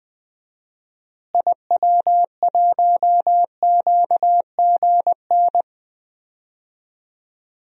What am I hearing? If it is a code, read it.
Morse code, heard as a keyed tone, about 20 wpm, 700 Hz. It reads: IW1QGN
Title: DXCC-100 Call Signs - 6 Characters - Repeated 2x - 20wpm